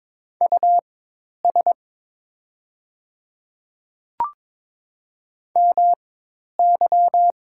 Morse code, heard as a keyed tone, about 22 wpm, spaced as for 10 wpm, 700 Hz. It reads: US MY
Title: 4x Character Spacing - Top 200 Words - 22wpm